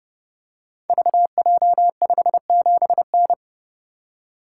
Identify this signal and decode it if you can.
Morse code, heard as a keyed tone, about 30 wpm, 700 Hz. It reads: VJ57N
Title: Code Groups - 5 Letter:Numbers - 30wpm